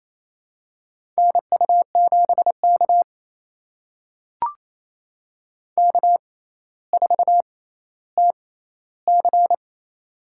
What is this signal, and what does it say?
Morse code, heard as a keyed tone, about 28 wpm, spaced as for 14 wpm, 700 Hz. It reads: NU7K K4TC
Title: USA Call Signs - 4 Characters - 6x Character Spacing - 28wpm